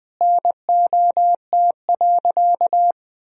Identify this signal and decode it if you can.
Morse code, heard as a keyed tone, about 20 wpm, 700 Hz. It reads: NOT.
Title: Sentences from Top 1000 Words - Review Part 3 - 20wpm